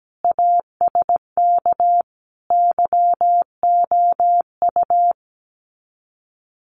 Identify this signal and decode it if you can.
Morse code, heard as a keyed tone, about 17 wpm, 700 Hz. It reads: ASK YOU
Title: Sets of 5 Words Encore2 17wpm